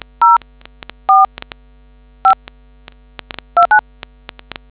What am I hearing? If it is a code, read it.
DTMF, heard as telephone touch tones, tones of about 110 ms. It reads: 04 5 29